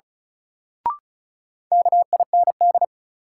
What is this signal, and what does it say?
Morse code, heard as a keyed tone, about 36 wpm, 700 Hz. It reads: KIND
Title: Sets of 2 Words - New Word Drill - 35wpm